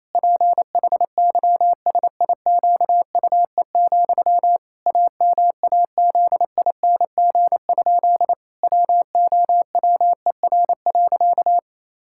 Morse code, heard as a keyed tone, about 28 wpm, 700 Hz. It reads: PHYSIQUE, AMAZING? WOWER.
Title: Pangram Challenge v2 - 28wpm